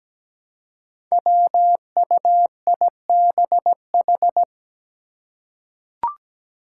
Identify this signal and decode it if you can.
Morse code, heard as a keyed tone, about 17 wpm, 700 Hz. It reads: WUIBH
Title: Five Challenging Letter-Number Combinations 17wpm